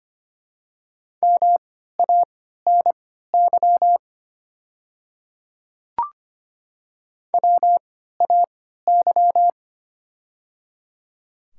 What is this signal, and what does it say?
Morse code, heard as a keyed tone, about 25 wpm, spaced as for 14 wpm, 700 Hz. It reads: MANY WAY
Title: 3x Character Spacing - Top 100 Words - 25wpm